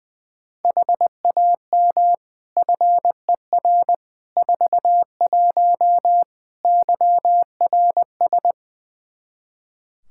Morse code, heard as a keyed tone, about 20 wpm, 700 Hz. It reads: HAM FER 41 YRS
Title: Practice QSO - Exchange 3 - More Info - 20wpm